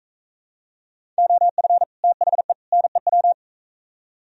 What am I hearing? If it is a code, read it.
Morse code, heard as a keyed tone, about 44 wpm, 700 Hz. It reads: OF THE NEW